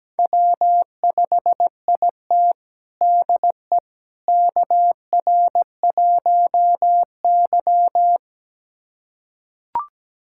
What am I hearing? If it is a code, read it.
Morse code, heard as a keyed tone, about 17 wpm, 700 Hz. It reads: W5IT DE KR1Y
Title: USA Callsign DE Callsign - 4 Characters - 17wpm